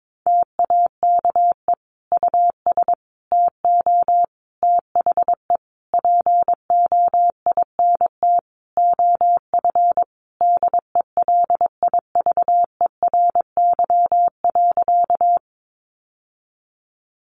Morse code, heard as a keyed tone, about 22 wpm, 700 Hz. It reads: TAKE US TO THE POINT OF DELIVERY.
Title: Sentences from Top 1500 Words 22wpm